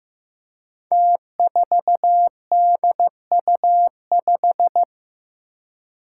Morse code, heard as a keyed tone, about 15 wpm, 700 Hz. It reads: T4DU5
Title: Five Challenging Letter-Number Combinations 15wpm